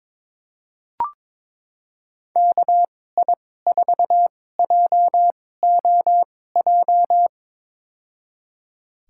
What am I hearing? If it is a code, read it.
Morse code, heard as a keyed tone, about 22 wpm, spaced as for 16 wpm, 700 Hz. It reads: KI4JOJ
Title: USA Call Signs - 6 Characters - 2x Character Spacing - 22wpm